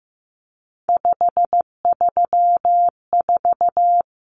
Morse code, heard as a keyed tone, about 15 wpm, 700 Hz. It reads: 534
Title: Common Cut Numbers - 3-Digit 15wpm